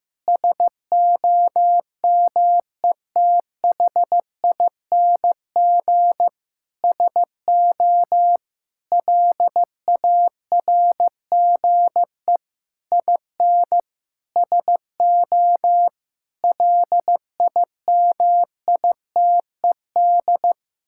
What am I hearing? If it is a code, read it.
Morse code, heard as a keyed tone, about 16 wpm, 700 Hz. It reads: SOMETHING SO LARGE IN SO LIMITED
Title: Sentences from Top 1000 Words - Review Part 6 - 15wpm